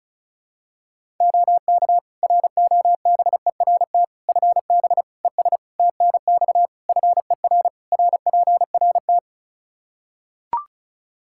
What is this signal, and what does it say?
Morse code, heard as a keyed tone, about 34 wpm, 700 Hz. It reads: OK ROBERT FB ES TNX FER RPRT
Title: K1USN SST QSO Practice 35wpm